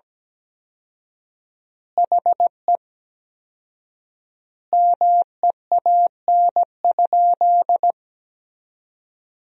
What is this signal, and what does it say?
Morse code, heard as a keyed tone, about 17 wpm, 700 Hz. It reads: HE MEAN?
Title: 4x Word-Spacing - Sentences from Top 200 Words - 17wpm